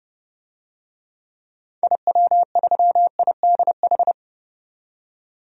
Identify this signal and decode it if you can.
Morse code, heard as a keyed tone, about 30 wpm, 700 Hz. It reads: IW3IDH
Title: DXCC-100 Call Signs - 6 Characters - 30wpm